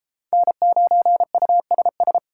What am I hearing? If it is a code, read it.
Morse code, heard as a keyed tone, about 30 wpm, 700 Hz. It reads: N9USS